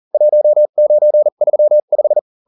Morse code, heard as a keyed tone, about 40 wpm, spaced as for 35 wpm, 600 Hz. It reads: 1935